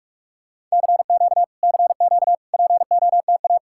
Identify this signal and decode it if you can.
Morse code, heard as a keyed tone, about 44 wpm, 700 Hz. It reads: CQ CQ POTA